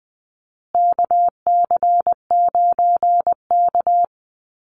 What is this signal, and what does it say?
Morse code, heard as a keyed tone, about 20 wpm, 700 Hz. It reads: KC9K
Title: Call Signs Repeated 2x - 20wpm